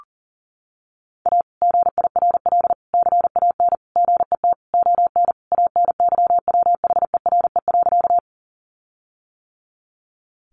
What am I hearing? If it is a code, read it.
Morse code, heard as a keyed tone, about 37 wpm, 700 Hz. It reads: A GIRL CAN GET ON ANYWHERE.
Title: Sentences from Top 1400 Words 40wpm